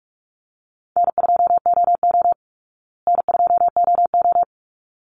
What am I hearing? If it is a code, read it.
Morse code, heard as a keyed tone, about 44 wpm, 700 Hz. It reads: N2OO N2OO
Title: Call Signs Repeated 3x - 45wpm